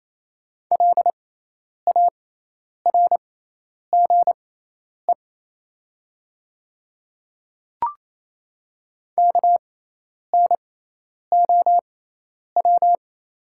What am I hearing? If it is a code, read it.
Morse code, heard as a keyed tone, about 28 wpm, spaced as for 10 wpm, 700 Hz. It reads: LARGE KNOW
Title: 6x Character Spacing - Top 300 Words - 28wpm